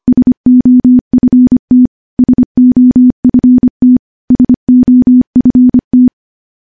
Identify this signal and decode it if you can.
Morse code, heard as a keyed tone, about 25 wpm, 250 Hz. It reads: SOFT SOFT SOFT
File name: morse-training-25WPM